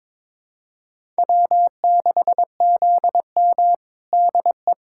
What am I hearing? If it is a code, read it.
Morse code, heard as a keyed tone, about 21 wpm, 700 Hz. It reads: W6ZM DE
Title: USA Callsign DE Callsign - 4 Characters - 22wpm